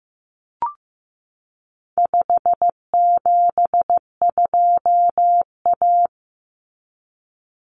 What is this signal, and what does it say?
Morse code, heard as a keyed tone, about 15 wpm, 700 Hz. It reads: 572A
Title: Common Cut Numbers - 4-Digit 15wpm